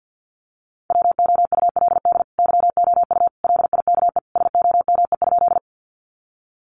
Morse code, heard as a keyed tone, about 50 wpm, 700 Hz. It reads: WOULD YOU LIKE SOME?
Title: Sentences from Top 100 Words 50wpm